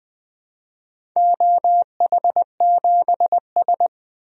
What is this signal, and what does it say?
Morse code, heard as a keyed tone, about 20 wpm, 700 Hz. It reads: OH7S
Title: DXCC-100 Call Signs - 4 Characters - 20wpm